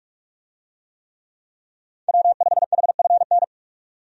Morse code, heard as a keyed tone, about 46 wpm, 700 Hz. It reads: W5HFN
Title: USA Call Signs - 5 Characters - 45wpm